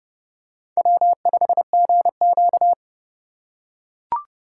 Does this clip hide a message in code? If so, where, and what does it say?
Yes: Morse code from 0.8 to 2.7 s, heard as a keyed tone, about 27 wpm, 700 Hz. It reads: W5GQ